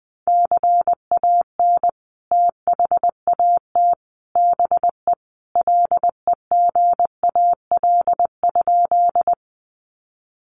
Morse code, heard as a keyed tone, about 20 wpm, 700 Hz. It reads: CAN THAT BE LEGAL?
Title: Sentences from Top 1200 Words 20wpm